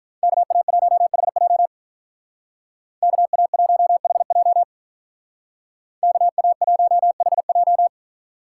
Morse code, heard as a keyed tone, about 43 wpm, 700 Hz. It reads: KA1HJ KA1HJ KA1HJ
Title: Speed Racing - USA 2x2 Call Signs 35wpm